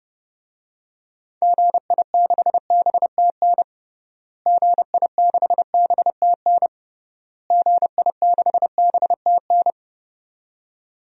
Morse code, heard as a keyed tone, about 32 wpm, 700 Hz. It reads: GI6BTN GI6BTN GI6BTN
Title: DXCC-100 Call Signs - 6 Characters - Repeated 3x - 30wpm